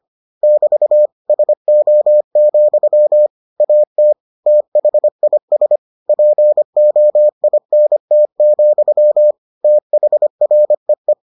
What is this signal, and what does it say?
Morse code, heard as a keyed tone, about 25 wpm, 600 Hz. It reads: <BT> SO, AT THIS POINT, THREE